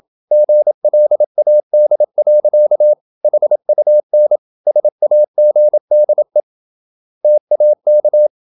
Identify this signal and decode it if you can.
Morse code, heard as a keyed tone, about 27 wpm, 600 Hz. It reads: GLAD. HUN SAGDE TAK
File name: Ferie QRQ 27wpm